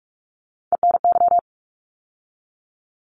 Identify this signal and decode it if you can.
Morse code, heard as a keyed tone, about 46 wpm, 700 Hz. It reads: ENY
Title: ARRL Sections 45wpm